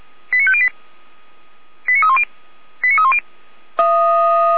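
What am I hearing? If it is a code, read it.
Five-tone selective call (ZVEI), heard as quick runs of five tones, tones of about 70 ms. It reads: 89508 89310 89310
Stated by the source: ZVEI five-tone selective call 89501 received on SDR, 455 channel.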